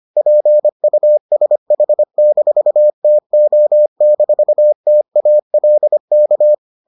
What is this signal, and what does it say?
Morse code, heard as a keyed tone, about 25 wpm, 600 Hz. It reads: PUSH-TO-TALK